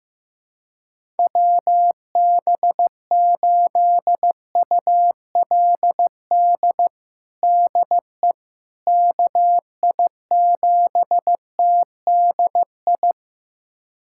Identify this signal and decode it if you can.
Morse code, heard as a keyed tone, about 15 wpm, 700 Hz. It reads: WB8ULD DE KI7TDI
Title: USA Callsign DE Callsign - 6 Characters - 15wpm